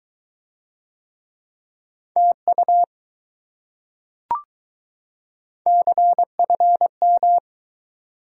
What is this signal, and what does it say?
Morse code, heard as a keyed tone, about 21 wpm, 700 Hz. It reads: TU CFM